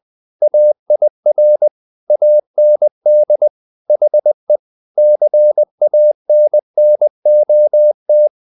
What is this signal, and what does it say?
Morse code, heard as a keyed tone, about 20 wpm, 600 Hz. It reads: AIR AND HE CANNOT
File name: the old man and the sea part 2 20wpm